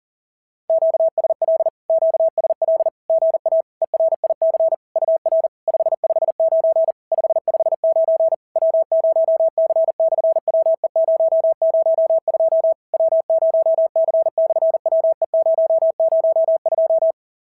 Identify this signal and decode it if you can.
Morse code, heard as a keyed tone, about 40 wpm, 690 Hz. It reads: QSL QSL GA ERIC UR 559 559 W0C/WE002 W0C/WE002
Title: SOTA QSO Practice 40wpm